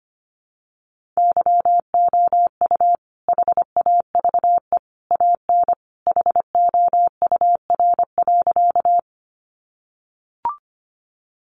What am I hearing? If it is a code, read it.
Morse code, heard as a keyed tone, about 25 wpm, 700 Hz. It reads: YOU HAVE AN HOUR.
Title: Sentences from Top 400 Words 25wpm